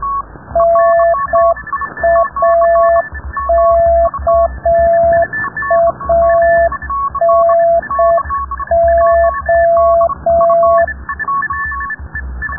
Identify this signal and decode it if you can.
Morse code, heard as a keyed tone, about 6 wpm, 690 Hz. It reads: NAKANO
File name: rubberst01 30 noise